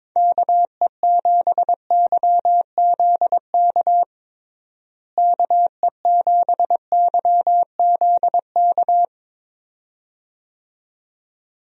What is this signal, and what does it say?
Morse code, heard as a keyed tone, about 22 wpm, 700 Hz. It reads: KE7YZK KE7YZK
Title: USA Call Signs - 6 Characters - Repeated 2x - 22wpm